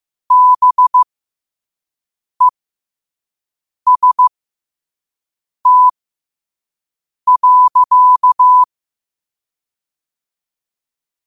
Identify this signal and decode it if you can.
Morse code, heard as a keyed tone, about 15 wpm, spaced as for 5 wpm, 1000 Hz. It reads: BEST.
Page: New quotes every day in morse code at 5 Words per minute.